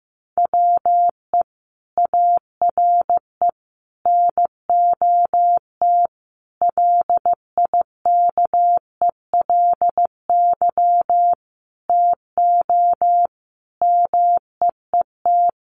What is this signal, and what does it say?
Morse code, heard as a keyed tone, about 15 wpm, 700 Hz. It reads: WE ARE NOT LIKELY TO MEET
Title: Sentences from Top 1100 Words 15wpm